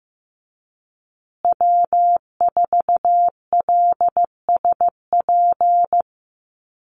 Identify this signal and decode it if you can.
Morse code, heard as a keyed tone, about 15 wpm, 700 Hz. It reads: W4LSP
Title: USA Call Signs Repeated 2x 15wpm